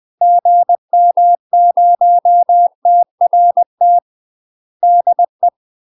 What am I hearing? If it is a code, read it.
Morse code, heard as a keyed tone, about 20 wpm, 700 Hz. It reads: GM0TRT DE
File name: UK MASTER QSOs TRAINING / 20WPM / UK MASTER QSOs 0073